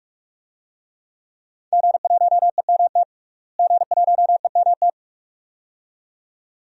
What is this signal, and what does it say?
Morse code, heard as a keyed tone, about 46 wpm, 700 Hz. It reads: G1EMT G1EMT
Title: DXCC-100 Call Signs - 5 Characters - Repeated 2x - 45wpm